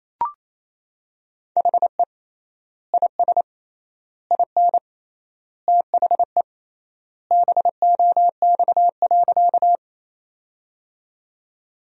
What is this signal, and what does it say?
Morse code, heard as a keyed tone, about 28 wpm, 700 Hz. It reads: HE IS IN THE BOX.
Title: 3x Word-Spacing - Sentences from Top 500 Words - 28wpm